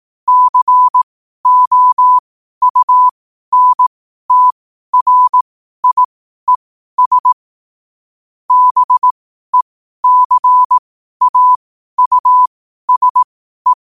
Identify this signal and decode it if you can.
Morse code, heard as a keyed tone, about 18 wpm, spaced as for 12 wpm, 1000 Hz. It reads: COUNTRIES BECAUSE
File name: qotdfiles / 12wpm